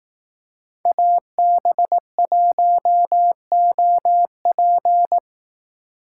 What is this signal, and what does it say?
Morse code, heard as a keyed tone, about 18 wpm, 700 Hz. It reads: AB1OP
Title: Speed Racing - USA 2x2 Call Signs 15wpm